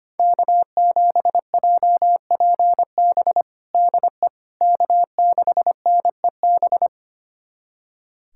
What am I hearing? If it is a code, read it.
Morse code, heard as a keyed tone, about 25 wpm, 700 Hz. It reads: K7JPB DE K6NEB
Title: USA Callsign DE Callsign 25wpm